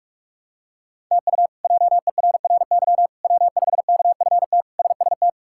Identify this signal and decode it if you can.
Morse code, heard as a keyed tone, about 46 wpm, 700 Hz. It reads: TU JERRY W5KFT SST
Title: K1USN SST QSO Practice - 45wpm